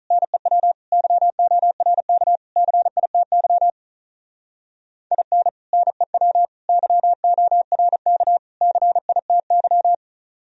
Morse code, heard as a keyed tone, about 46 wpm, 700 Hz. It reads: NEW YORK CITY IN NEW YORK CITY